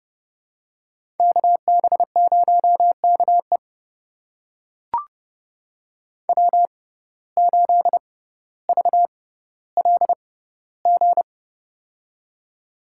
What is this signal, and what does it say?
Morse code, heard as a keyed tone, about 30 wpm, spaced as for 15 wpm, 690 Hz. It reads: KB0KE W8VLG
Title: USA Call Signs - 5 Characters - 6x Character Spacing - 30wpm